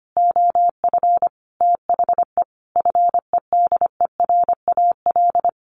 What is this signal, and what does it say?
Morse code, heard as a keyed tone, about 25 wpm, 700 Hz. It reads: OF THE FEDERAL
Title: Sentences from Top 1400 Words 25wpm